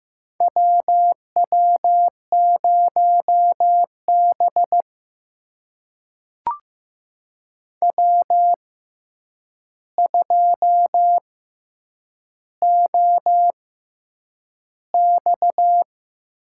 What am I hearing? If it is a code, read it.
Morse code, heard as a keyed tone, about 15 wpm, spaced as for 8 wpm, 700 Hz. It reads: WW0B W2OX
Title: USA Call Signs - 4 Characters - 6x Character Spacing - 15wpm